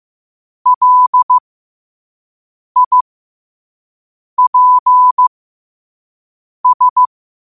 Morse code, heard as a keyed tone, about 15 wpm, spaced as for 5 wpm, 1000 Hz. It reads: LIPS
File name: qotdfiles / 5wpm